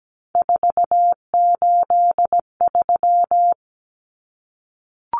Morse code, heard as a keyed tone, about 17 wpm, 700 Hz. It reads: 483
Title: Numbers - 3-Digit 17wpm